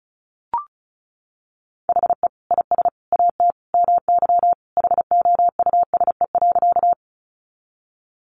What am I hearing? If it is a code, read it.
Morse code, heard as a keyed tone, about 34 wpm, 700 Hz. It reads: HE IS AT MY HOUSE.
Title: Sentences from Top 200 Words 35wpm